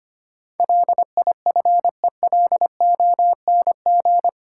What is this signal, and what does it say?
Morse code, heard as a keyed tone, about 25 wpm, 700 Hz. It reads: LIFELONG
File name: Eight Letter Words - 25wpm